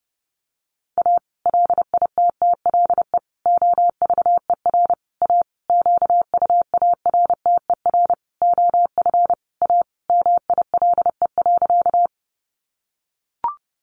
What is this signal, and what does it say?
Morse code, heard as a keyed tone, about 30 wpm, 700 Hz. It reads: A LITTLE OVER A QUARTER OF A MILE.
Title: Sentences from Top 1300 Words 30wpm